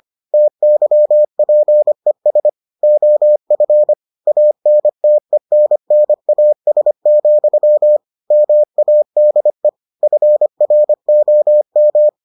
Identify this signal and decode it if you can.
Morse code, heard as a keyed tone, about 25 wpm, 600 Hz. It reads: TYPES OF ANTENNAS, MADE FROM